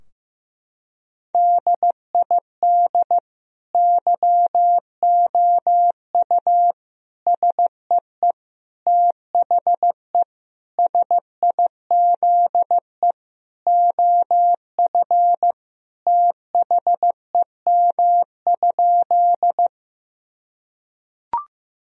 Morse code, heard as a keyed tone, about 15 wpm, 700 Hz. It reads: DID YOU SEE THE SIZE OF THEM?
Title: Sentences from Top 1000 Words - Review Part 2 - 15wpm